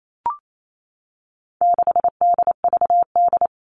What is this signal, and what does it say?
Morse code, heard as a keyed tone, about 28 wpm, 700 Hz. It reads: 6DVD
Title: All Cut Numbers - 4-Digit 28wpm